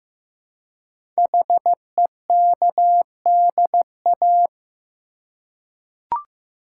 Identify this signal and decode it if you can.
Morse code, heard as a keyed tone, about 15 wpm, 700 Hz. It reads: HEKDA
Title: Code Groups - 5 Letters - 15wpm